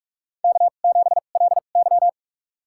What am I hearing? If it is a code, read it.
Morse code, heard as a keyed tone, about 46 wpm, spaced as for 32 wpm, 700 Hz. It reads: K7LY
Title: USA Call Signs - 4 Characters - 2x Character Spacing - 45wpm